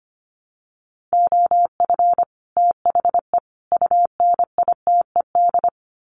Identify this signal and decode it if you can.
Morse code, heard as a keyed tone, about 25 wpm, 700 Hz. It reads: OF THE UNITED
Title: Sets of 3 Words 25wpm